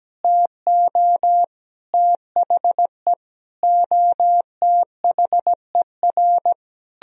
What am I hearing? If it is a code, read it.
Morse code, heard as a keyed tone, about 17 wpm, 700 Hz. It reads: TO THE OTHER
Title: Sentences from Top 1000 Words - Review Part 4 - 17wpm